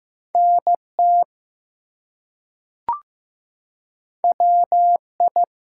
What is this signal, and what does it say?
Morse code, heard as a keyed tone, about 15 wpm, 700 Hz. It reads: NT WI
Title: US States and Canadian Provinces 15wpm